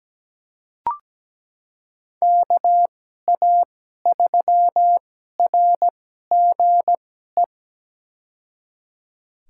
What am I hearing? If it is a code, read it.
Morse code, heard as a keyed tone, about 17 wpm, spaced as for 12 wpm, 700 Hz. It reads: KA3RGE
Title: USA Call Signs - 2x Character Spacing - 17wpm